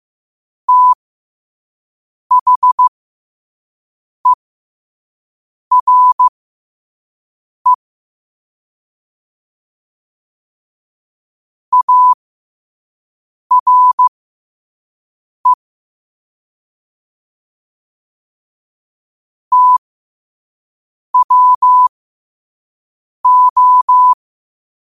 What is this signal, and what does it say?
Morse code, heard as a keyed tone, about 15 wpm, spaced as for 5 wpm, 1000 Hz. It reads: THERE ARE TWO
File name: qotdfiles / 5wpm